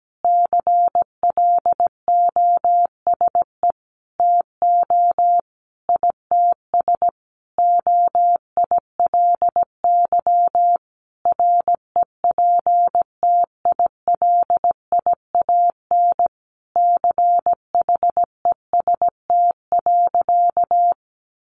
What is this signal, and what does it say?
Morse code, heard as a keyed tone, about 17 wpm, 700 Hz. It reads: CLOSE TO ITS OILY REPTILIAN CHEST.
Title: A Sound Of Thunder 17wpm